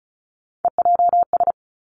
Morse code, heard as a keyed tone, about 36 wpm, 700 Hz. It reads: EJS
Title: Three Challenging Letter-Number Combinations 35wpm